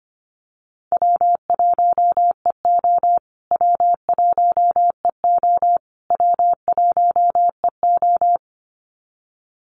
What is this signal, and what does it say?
Morse code, heard as a keyed tone, about 25 wpm, 700 Hz. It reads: W1EO W1EO W1EO
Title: USA Call Signs Repeated 3x 25wpm